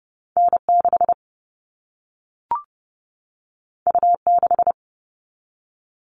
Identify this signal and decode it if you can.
Morse code, heard as a keyed tone, about 30 wpm, 700 Hz. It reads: N6 U6
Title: All Cut Numbers - 2-Digit 30wpm